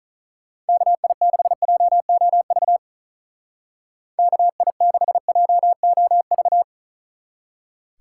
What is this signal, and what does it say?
Morse code, heard as a keyed tone, about 36 wpm, 700 Hz. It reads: KI6JOV KI6JOV
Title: Speed Racing - USA Call Signs 35wpm